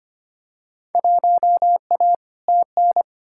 Morse code, heard as a keyed tone, about 25 wpm, 700 Hz. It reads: 1A TN
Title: Field Day Exchanges - 25wpm